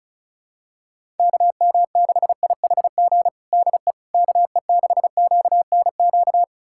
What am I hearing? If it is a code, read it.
Morse code, heard as a keyed tone, about 34 wpm, 700 Hz. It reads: KM6IHG DE KE6QNQ
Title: USA Callsign DE Callsign - 6 Characters - 35wpm